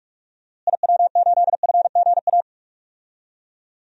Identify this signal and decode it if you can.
Morse code, heard as a keyed tone, about 46 wpm, 700 Hz. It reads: IW8FGA
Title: DXCC-100 Call Signs - 6 Characters - Repeated 3x - 45wpm